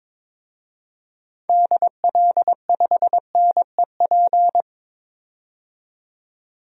Morse code, heard as a keyed tone, about 22 wpm, 700 Hz. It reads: DL5NEP
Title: DXCC-100 Call Signs - 6 Characters - 22wpm